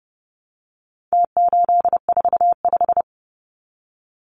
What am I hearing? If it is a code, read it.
Morse code, heard as a keyed tone, about 30 wpm, 700 Hz. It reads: T845
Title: Numbers - Contest Style - 4-Digit 30wpm